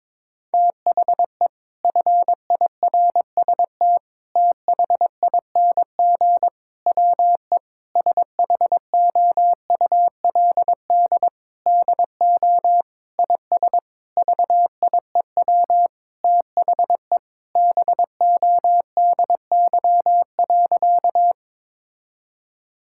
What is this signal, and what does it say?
Morse code, heard as a keyed tone, about 22 wpm, 700 Hz. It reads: THE FIRST THING WE SHOULD DO IS VIEW THE BODY.
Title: Sentences from Top 1000 Words - Review Part 6 - 22wpm